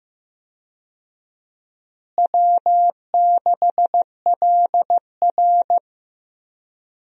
Morse code, heard as a keyed tone, about 15 wpm, 700 Hz. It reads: W6LR
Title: USA Call Signs - 4 Characters - Repeated 3x - 15wpm